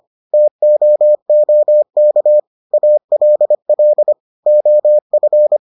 Morse code, heard as a keyed tone, about 25 wpm, 600 Hz. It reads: TOOK ALL OF